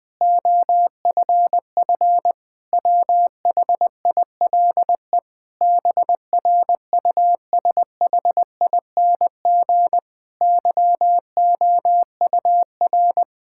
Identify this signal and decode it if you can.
Morse code, heard as a keyed tone, about 20 wpm, 700 Hz. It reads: OFF WHILE BRUSHING YOUR
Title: Difficult Sentences with Hints - 20wpm